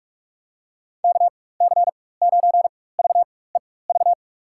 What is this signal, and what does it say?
Morse code, heard as a keyed tone, about 46 wpm, spaced as for 21 wpm, 700 Hz. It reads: KC9VEV
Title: USA Call Signs - 6 Characters - 4x Character Spacing - 45wpm